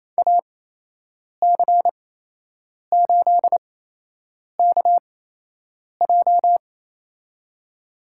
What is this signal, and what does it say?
Morse code, heard as a keyed tone, about 28 wpm, spaced as for 8 wpm, 700 Hz. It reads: AC8KJ
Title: USA Call Signs - 8x Character Spacing - 28wpm